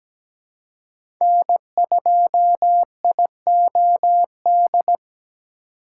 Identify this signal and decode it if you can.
Morse code, heard as a keyed tone, about 17 wpm, 700 Hz. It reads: N2IOD
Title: USA Call Signs - 2x Character Spacing - 17wpm